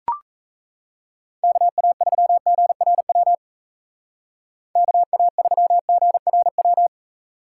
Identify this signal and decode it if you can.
Morse code, heard as a keyed tone, about 37 wpm, 700 Hz. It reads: KA3GRW KA3GRW